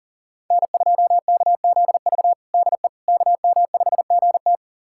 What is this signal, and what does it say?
Morse code, heard as a keyed tone, about 40 wpm, 700 Hz. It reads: N2KZV DE KM5GT
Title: USA Callsign DE Callsign - 5 Characters - 40wpm